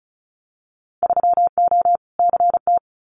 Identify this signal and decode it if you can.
Morse code, heard as a keyed tone, about 36 wpm, 700 Hz. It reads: 3O CT
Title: Winter Field Day Exchanges Encore 35wpm